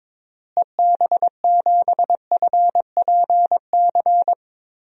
Morse code, heard as a keyed tone, about 22 wpm, 700 Hz. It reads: EB7FPC
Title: DXCC-100 Call Signs - 6 Characters - Repeated 3x - 22wpm